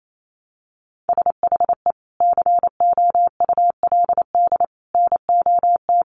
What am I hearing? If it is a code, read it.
Morse code, heard as a keyed tone, about 28 wpm, 700 Hz. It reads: SHE COULD NOT
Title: Sentences from Top 1400 Words 28wpm